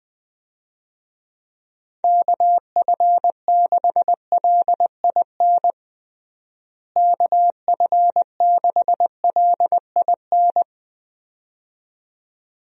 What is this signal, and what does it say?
Morse code, heard as a keyed tone, about 21 wpm, 700 Hz. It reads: KF6LIN KF6LIN
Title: USA Call Signs - 6 Characters - Repeated 2x - 20wpm